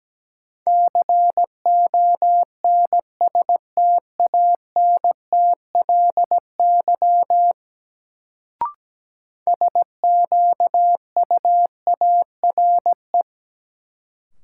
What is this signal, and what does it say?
Morse code, heard as a keyed tone, about 17 wpm, 700 Hz. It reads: CONSTANTLY SQUARE
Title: Sentences from Top 1000 Words - Review Part 7 - New Word Drill - 17wpm